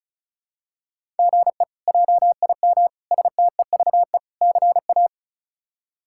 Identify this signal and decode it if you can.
Morse code, heard as a keyed tone, about 36 wpm, 700 Hz. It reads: GE JIM STEVE CA
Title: K1USN SST QSO Practice - 35wpm